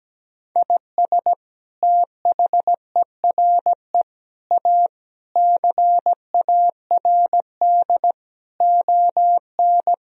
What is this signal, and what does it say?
Morse code, heard as a keyed tone, about 17 wpm, 700 Hz. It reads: IS THERE A CARD ON
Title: Sentences from Top 1000 Words - Review Part 4 - 17wpm